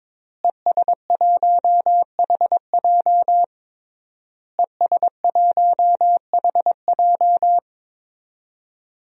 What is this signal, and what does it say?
Morse code, heard as a keyed tone, about 22 wpm, 700 Hz. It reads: ES1HJ ES1HJ
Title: DXCC-100 Call Signs - 5 Characters - Repeated 3x - 22wpm